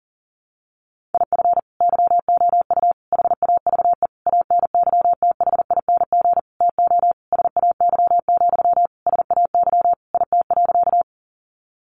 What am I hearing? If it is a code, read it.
Morse code, heard as a keyed tone, about 40 wpm, 700 Hz. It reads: IF YOU HAVE ANYTHING TO SAY, SAY IT.
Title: Sentences from Top 1000 Words 40wpm